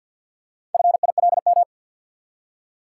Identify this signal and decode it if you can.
Morse code, heard as a keyed tone, about 49 wpm, 700 Hz. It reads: FILM
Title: Four Letter Words - 50wpm